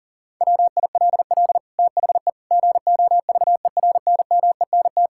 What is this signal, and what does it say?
Morse code, heard as a keyed tone, about 37 wpm, 700 Hz. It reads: WILL THE GOVERNMENT
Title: Sentences from Top 1000 Words - Review Part 5 - 40wpm